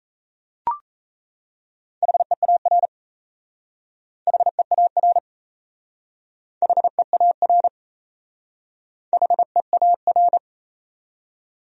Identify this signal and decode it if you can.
Morse code, heard as a keyed tone, about 35 wpm, 700 Hz. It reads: HEAR HEAR HEAR HEAR
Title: Speed-Racing Top 500 Words 28wpm